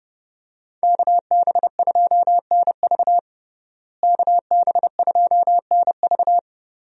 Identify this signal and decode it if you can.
Morse code, heard as a keyed tone, about 30 wpm, 700 Hz. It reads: KB2NV KB2NV
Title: USA Call Signs - 5 Characters - Repeated 2x - 30wpm